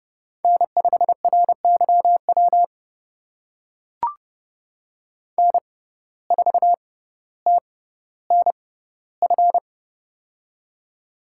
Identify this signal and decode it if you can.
Morse code, heard as a keyed tone, about 32 wpm, spaced as for 16 wpm, 700 Hz. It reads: N5RYW N4TNF
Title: USA Call Signs - 5 Characters - 6x Character Spacing - 30wpm